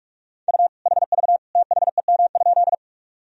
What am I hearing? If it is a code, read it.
Morse code, heard as a keyed tone, about 46 wpm, 700 Hz. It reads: U HV THEM?
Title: Sentences from Top 100 Words and Abbreviations 45wpm